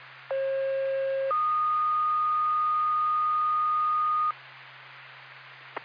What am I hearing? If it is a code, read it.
Two-tone page, heard as a one-second tone, then a three-second tone, about 540 then 1230 Hz. It is heard from 0.3 to 4.3 s.